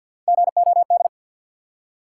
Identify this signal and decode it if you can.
Morse code, heard as a keyed tone, about 52 wpm, 700 Hz. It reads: GOD
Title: Three Letter Words - 50wpm